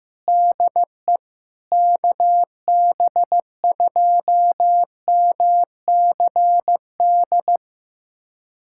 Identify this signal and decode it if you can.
Morse code, heard as a keyed tone, about 15 wpm, 700 Hz. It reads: DE KB2MCD
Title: USA Callsign DE Callsign 15wpm